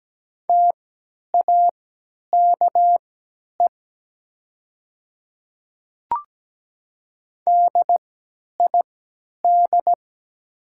Morse code, heard as a keyed tone, about 17 wpm, spaced as for 10 wpm, 700 Hz. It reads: TAKE DID
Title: 3x Character Spacing - Top 200 Words - 17wpm